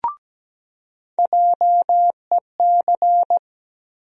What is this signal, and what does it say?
Morse code, heard as a keyed tone, about 17 wpm, 700 Hz. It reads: JEC
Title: USA Call Sign Postfixes - 3 Letters 17wpm